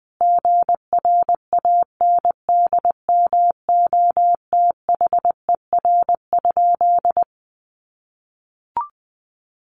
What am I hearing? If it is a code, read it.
Morse code, heard as a keyed tone, about 20 wpm, 700 Hz. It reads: GRANDMOTHER?
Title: Sentences from Top 1200 Words 20wpm